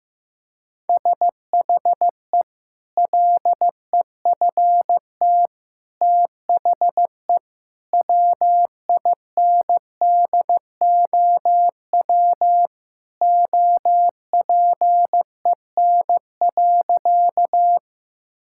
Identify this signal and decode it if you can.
Morse code, heard as a keyed tone, about 16 wpm, 700 Hz. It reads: SHE LEFT THE WINDOW OPEN.
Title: Sentences from Top 1600 Words 15wpm